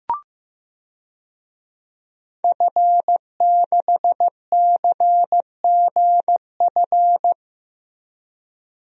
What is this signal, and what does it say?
Morse code, heard as a keyed tone, about 15 wpm, 700 Hz. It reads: F6CGF
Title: DXCC-100 Call Signs - 5 Characters - Repeated 2x - 15wpm